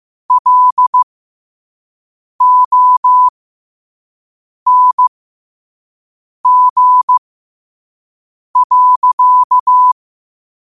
Morse code, heard as a keyed tone, about 15 wpm, spaced as for 5 wpm, 1000 Hz. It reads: LONG.
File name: qotdfiles / 5wpm